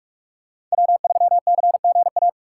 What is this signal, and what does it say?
Morse code, heard as a keyed tone, about 43 wpm, 700 Hz. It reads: W3CGA